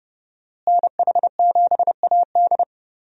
Morse code, heard as a keyed tone, about 30 wpm, 700 Hz. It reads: NH7AD
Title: USA Call Signs - 5 Characters - Repeated 3x - 30wpm